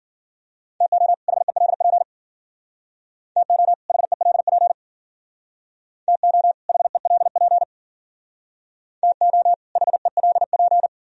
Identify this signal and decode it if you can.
Morse code, heard as a keyed tone, about 53 wpm, 700 Hz. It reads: TO HELP TO HELP TO HELP TO HELP